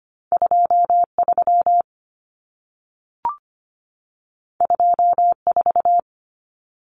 Morse code, heard as a keyed tone, about 25 wpm, 700 Hz. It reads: 23 24
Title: Common Cut Numbers - 2-Digit 25wpm